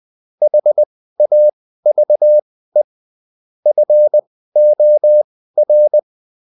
Morse code, heard as a keyed tone, about 20 wpm, spaced as for 14 wpm, 600 Hz. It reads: HAVE FOR